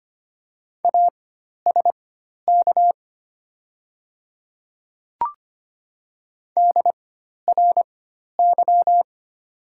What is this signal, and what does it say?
Morse code, heard as a keyed tone, about 25 wpm, spaced as for 12 wpm, 700 Hz. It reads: ASK DRY
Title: 4x Character Spacing - Top 500 Words - 25wpm